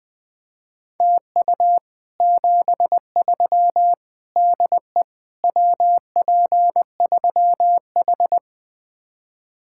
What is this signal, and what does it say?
Morse code, heard as a keyed tone, about 20 wpm, 700 Hz. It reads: TU 73 DE WP3H
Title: Speed-Racing Common QSO Phrases Encore - 20wpm